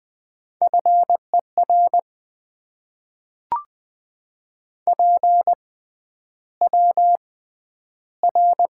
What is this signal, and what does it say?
Morse code, heard as a keyed tone, about 20 wpm, spaced as for 10 wpm, 700 Hz. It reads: FER PWR
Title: Top 90 QSO Elements - 6x Character Spacing - 20wpm